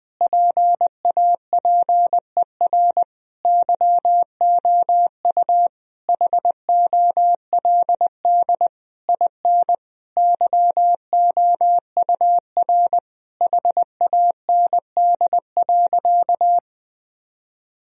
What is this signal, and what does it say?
Morse code, heard as a keyed tone, about 20 wpm, 700 Hz. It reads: PAPER YOU HOLD IN YOUR HAND.
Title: Sentences from Top 1000 Words - Review Part 2 - 20wpm